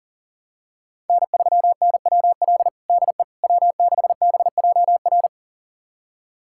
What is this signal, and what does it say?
Morse code, heard as a keyed tone, about 40 wpm, 700 Hz. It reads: N3NWL DE W6BJR
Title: USA Callsign DE Callsign - 5 Characters - 40wpm